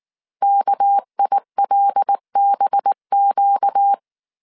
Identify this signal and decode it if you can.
Morse code, heard as a keyed tone, about 19 wpm, 800 Hz. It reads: KILBQ
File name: CW / DET / 1 6 1500C S C 18wpm 2356